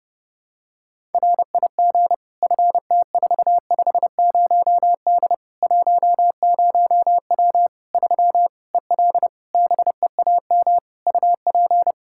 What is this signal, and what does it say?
Morse code, heard as a keyed tone, about 32 wpm, 700 Hz. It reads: RIG FT450D 10W 3 EL BEAM UP
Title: Practice QSO - Exchange 2 - Info - 30wpm